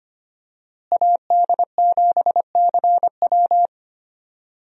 Morse code, heard as a keyed tone, about 25 wpm, 700 Hz. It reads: AD7CW
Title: USA Call Signs - 5 Characters - Repeated 3x - 25wpm